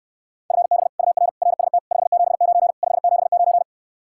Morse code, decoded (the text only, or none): FB UR RST 579 579